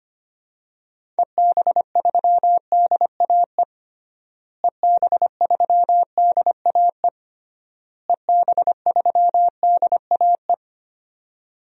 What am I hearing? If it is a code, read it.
Morse code, heard as a keyed tone, about 25 wpm, 700 Hz. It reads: EB3DAE EB3DAE EB3DAE
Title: DXCC-100 Call Signs - 6 Characters - Repeated 3x - 25wpm